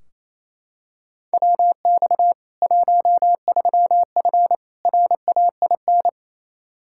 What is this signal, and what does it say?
Morse code, heard as a keyed tone, about 28 wpm, 700 Hz. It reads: WX 13F RAIN
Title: Practice QSO - Exchange 2 - Info - 28wpm